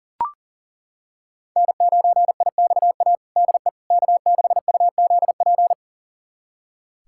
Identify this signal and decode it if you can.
Morse code, heard as a keyed tone, about 40 wpm, 700 Hz. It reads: N9IXA DE K6UZP
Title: USA Callsign DE Callsign - 5 Characters - 40wpm